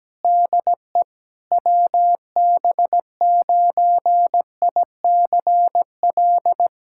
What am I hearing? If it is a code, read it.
Morse code, heard as a keyed tone, about 17 wpm, 700 Hz. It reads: DE WB9ICL
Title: SOTA QSO Practice 17wpm